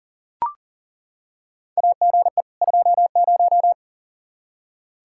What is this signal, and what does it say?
Morse code, heard as a keyed tone, about 38 wpm, 700 Hz. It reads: AGE 20
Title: Practice QSO - Exchange 3 - More Info - 40wpm